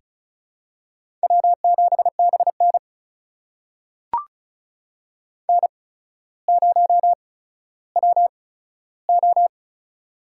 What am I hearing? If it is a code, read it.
Morse code, heard as a keyed tone, about 36 wpm, spaced as for 15 wpm, 700 Hz. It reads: W7BN N0WO
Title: USA Call Signs - 4 Characters - 8x Character Spacing - 35wpm